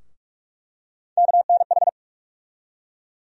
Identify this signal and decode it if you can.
Morse code, heard as a keyed tone, about 45 wpm, 700 Hz. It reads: KNH
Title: USA Call Sign Postfixes - 3 Letters 45wpm